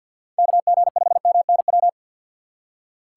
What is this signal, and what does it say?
Morse code, heard as a keyed tone, about 53 wpm, 700 Hz. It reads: KG5MNW
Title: USA Call Signs - 6 Characters - 3x Character Spacing - 50wpm